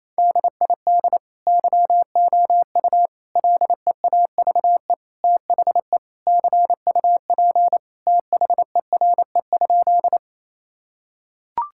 Morse code, heard as a keyed tone, about 28 wpm, 700 Hz. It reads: DID YOU LEAVE THE CUP THERE?
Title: Sentences from Top 1600 Words 28wpm